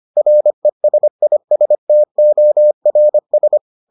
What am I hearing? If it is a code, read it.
Morse code, heard as a keyed tone, about 25 wpm, 600 Hz. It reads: RESISTORS